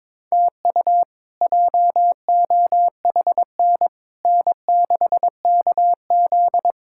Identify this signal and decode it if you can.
Morse code, heard as a keyed tone, about 22 wpm, 700 Hz. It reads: TU JOHN N6KZ
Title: K1USN SST QSO Practice - 22wpm